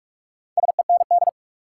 Morse code, heard as a keyed tone, about 42 wpm, 700 Hz. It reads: SEND
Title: Sentences from Top 1500 Words - New Word Drill Alt - 45wpm